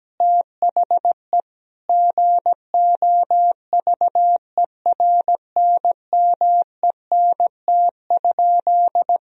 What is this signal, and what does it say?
Morse code, heard as a keyed tone, about 17 wpm, 700 Hz. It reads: THE GOVERNMENT?
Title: Sentences from Top 1000 Words - Review Part 4 - 17wpm